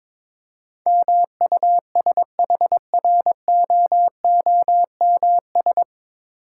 Morse code, heard as a keyed tone, about 22 wpm, 690 Hz. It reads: MUSHROOMS
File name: Difficult Words with Hints - 22wpm